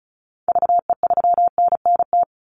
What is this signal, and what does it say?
Morse code, heard as a keyed tone, about 34 wpm, 700 Hz. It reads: VE3NNT
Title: Call Signs - 35wpm